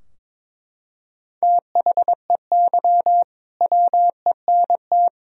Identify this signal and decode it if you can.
Morse code, heard as a keyed tone, about 22 wpm, 700 Hz. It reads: THEY WENT
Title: Sentences from Top 1000 Words - Review Part 2 - 22wpm